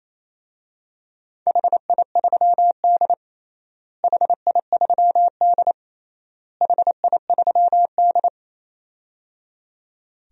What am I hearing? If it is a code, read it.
Morse code, heard as a keyed tone, about 28 wpm, 700 Hz. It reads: HI3D HI3D HI3D
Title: DXCC-100 Call Signs - 4 Characters - Repeated 3x - 28wpm